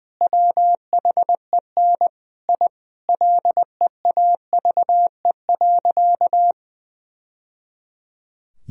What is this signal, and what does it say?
Morse code, heard as a keyed tone, about 20 wpm, 700 Hz. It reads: WHEN I LEAVE.
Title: Sentences from Top 1000 Words - Review Part 2 - 20wpm